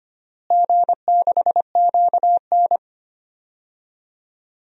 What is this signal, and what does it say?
Morse code, heard as a keyed tone, about 25 wpm, 700 Hz. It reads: G6QN
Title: DXCC-100 Call Signs - 4 Characters - Repeated 3x - 25wpm